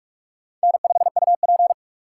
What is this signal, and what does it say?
Morse code, heard as a keyed tone, about 46 wpm, 700 Hz. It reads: N5UP
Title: USA Call Signs - 6x Character Spacing - 45wpm